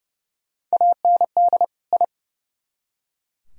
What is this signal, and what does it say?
Morse code, heard as a keyed tone, about 40 wpm, 700 Hz. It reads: AND I